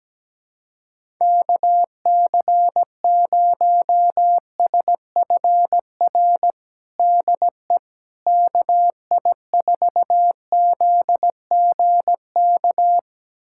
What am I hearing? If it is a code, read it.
Morse code, heard as a keyed tone, about 17 wpm, 700 Hz. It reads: KC0SFR DE KI4ZGK
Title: USA Callsign DE Callsign - 6 Characters - 17wpm